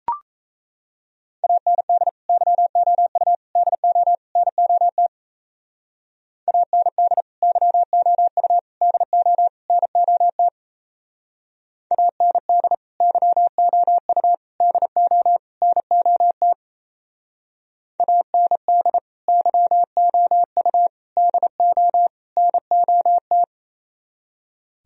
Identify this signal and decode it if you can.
Morse code, heard as a keyed tone, about 34 wpm, 700 Hz. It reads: AND YOU DO NOT AND YOU DO NOT AND YOU DO NOT AND YOU DO NOT
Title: Speed-Racing - Sets of 4 Words 28wpm